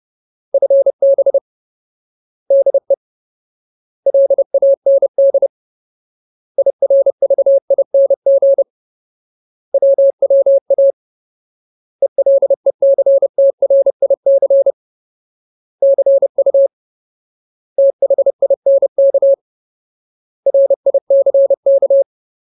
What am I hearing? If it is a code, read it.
Morse code, heard as a keyed tone, about 30 wpm, 550 Hz. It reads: FB DE LAND IRVING WWA ELECTRIC CU THINK RICK
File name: Ultimate Word List / 30WPM Ultimate